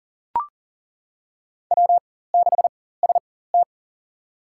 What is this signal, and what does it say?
Morse code, heard as a keyed tone, about 40 wpm, spaced as for 19 wpm, 700 Hz. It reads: W6ST
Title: USA Call Signs - 4 Characters - 4x Character Spacing - 40wpm